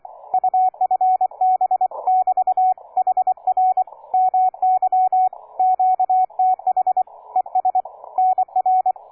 Morse code, heard as a keyed tone, about 27 wpm, 740 Hz, with radio noise. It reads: UFB <BT> HR MY QTH ES NR
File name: cw forum / K9DX 1818 le 17-12-2008 05h47